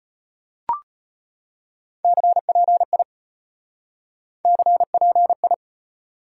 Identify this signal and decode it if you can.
Morse code, heard as a keyed tone, about 33 wpm, 700 Hz. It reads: CPI CPI